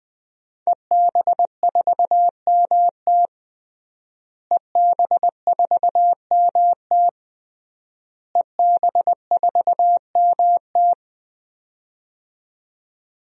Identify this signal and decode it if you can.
Morse code, heard as a keyed tone, about 21 wpm, 700 Hz. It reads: EB4MT EB4MT EB4MT
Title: DXCC-100 Call Signs - 5 Characters - Repeated 3x - 20wpm